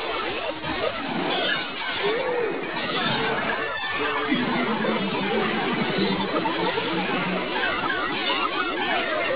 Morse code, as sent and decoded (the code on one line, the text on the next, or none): none